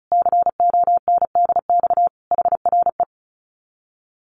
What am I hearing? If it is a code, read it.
Morse code, heard as a keyed tone, about 34 wpm, 700 Hz. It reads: CONDX HRE
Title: Common QSO Phrases 35wpm